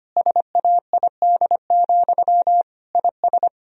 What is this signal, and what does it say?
Morse code, heard as a keyed tone, about 25 wpm, 700 Hz. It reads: SAID, IS
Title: Sentences from Top 1000 Words - Review Part 4 - 25wpm